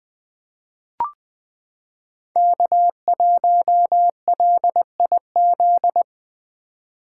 Morse code, heard as a keyed tone, about 20 wpm, 700 Hz. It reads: K1LIZ
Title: 2023 SOTA Chaser Call Signs - 20wpm